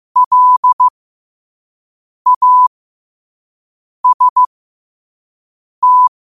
Morse code, heard as a keyed tone, about 15 wpm, spaced as for 5 wpm, 1000 Hz. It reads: LAST
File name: qotdfiles / 5wpm